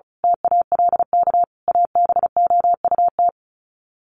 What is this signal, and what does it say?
Morse code, heard as a keyed tone, about 34 wpm, 700 Hz. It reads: TALK ABOUT
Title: Sets of 5 Words 35wpm